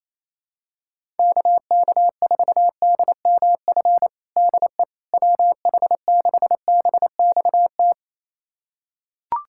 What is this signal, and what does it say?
Morse code, heard as a keyed tone, about 27 wpm, 700 Hz. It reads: KK4DMF DE WH6BXT
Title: USA Callsign DE Callsign 28wpm